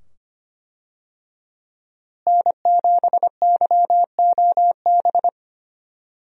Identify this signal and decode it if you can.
Morse code, heard as a keyed tone, about 25 wpm, 700 Hz. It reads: N7YOB
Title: USA Call Signs - 5 Characters - Repeated 2x - 25wpm